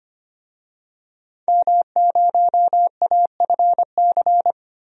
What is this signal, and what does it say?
Morse code, heard as a keyed tone, about 25 wpm, 700 Hz. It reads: M0AFC
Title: DXCC-100 Call Signs - 5 Characters - 25wpm